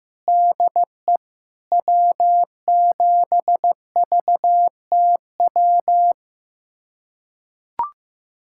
Morse code, heard as a keyed tone, about 15 wpm, 700 Hz. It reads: DE W7VTW
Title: USA Callsign DE Callsign 15wpm